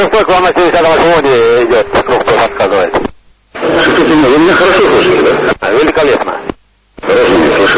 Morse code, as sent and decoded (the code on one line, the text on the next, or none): none